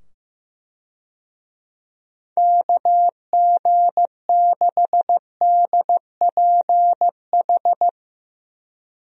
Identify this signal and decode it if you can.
Morse code, heard as a keyed tone, about 15 wpm, 700 Hz. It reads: KG6DPH
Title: USA Call Signs - 6 Characters - 15wpm